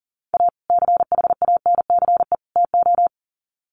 Morse code, heard as a keyed tone, about 37 wpm, 700 Hz. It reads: A CHANCE TO
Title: Sets of 3 Words 40wpm